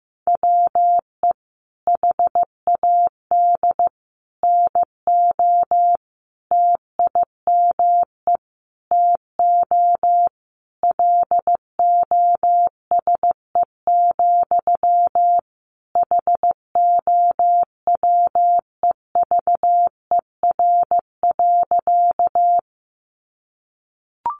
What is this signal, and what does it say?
Morse code, heard as a keyed tone, about 15 wpm, 700 Hz. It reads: WE HAD NO TIME TO LOSE, HOWEVER.
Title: Sentences from Top 1100 Words 15wpm